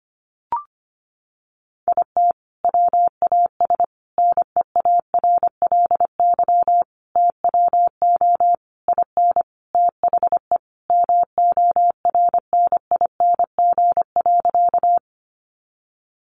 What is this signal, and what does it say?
Morse code, heard as a keyed tone, about 25 wpm, 700 Hz. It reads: IT WAS NEARLY TWO IN THE MORNING.
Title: Sentences from Top 1300 Words 25wpm